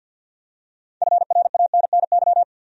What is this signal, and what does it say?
Morse code, heard as a keyed tone, about 52 wpm, 700 Hz. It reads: FRANNY
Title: USA Common Names - 50wpm